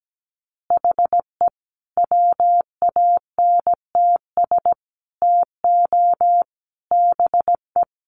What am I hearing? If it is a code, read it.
Morse code, heard as a keyed tone, about 17 wpm, 700 Hz. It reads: HE WANTS TO BE
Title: Additional Easy Phrases 17wpm